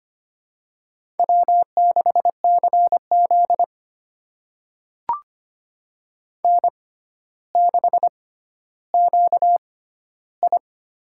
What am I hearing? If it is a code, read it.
Morse code, heard as a keyed tone, about 25 wpm, spaced as for 13 wpm, 700 Hz. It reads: W6CZ N6QI
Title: USA Call Signs - 4 Characters - 6x Character Spacing - 25wpm